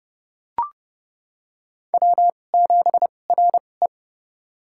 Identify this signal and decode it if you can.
Morse code, heard as a keyed tone, about 30 wpm, spaced as for 22 wpm, 700 Hz. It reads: W7RE
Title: USA Call Signs - 4 Characters - 2x Character Spacing - 30wpm